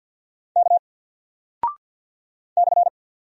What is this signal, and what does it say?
Morse code, heard as a keyed tone, about 50 wpm, 700 Hz. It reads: K /
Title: Single Character (Letter:Number:Punctuation:Prosign) 50wpm